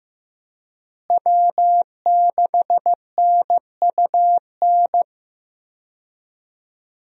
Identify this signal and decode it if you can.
Morse code, heard as a keyed tone, about 15 wpm, 700 Hz. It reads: W6NUN
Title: USA Call Signs - 5 Characters - Repeated 2x - 15wpm